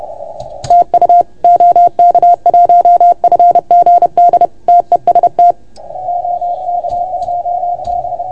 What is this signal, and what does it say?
Morse code, heard as a keyed tone, about 30 wpm, 690 Hz. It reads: TU OK1FGD TEST OK2B
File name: Archiv / CUC381 / OK1FGD CUC 381